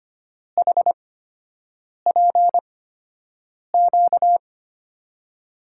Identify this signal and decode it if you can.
Morse code, heard as a keyed tone, about 25 wpm, spaced as for 7 wpm, 700 Hz. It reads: HPQ
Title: USA Call Signs - 8x Character Spacing - 25wpm